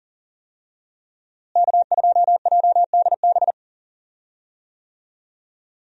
Morse code, heard as a keyed tone, about 37 wpm, 700 Hz. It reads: K2JDB
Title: USA Call Signs - 5 Characters - Repeated 3x - 40wpm